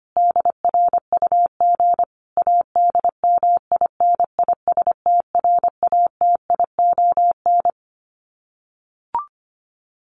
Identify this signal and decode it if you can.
Morse code, heard as a keyed tone, about 25 wpm, 700 Hz. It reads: DRUG ADMINISTRATION
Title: Sets of 5 Words 25wpm